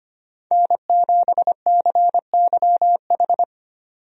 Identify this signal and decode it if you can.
Morse code, heard as a keyed tone, about 25 wpm, 700 Hz. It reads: N7CYH
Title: USA Call Signs - 5 Characters - 25wpm